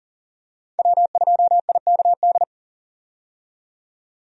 Morse code, heard as a keyed tone, about 40 wpm, 700 Hz. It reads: W2IKD
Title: USA Call Signs - 5 Characters - 40wpm